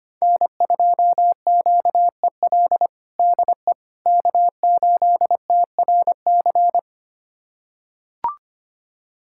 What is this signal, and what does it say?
Morse code, heard as a keyed tone, about 25 wpm, 700 Hz. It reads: N2QEL DE K8TRC
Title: USA Callsign DE Callsign - 5 Characters - 25wpm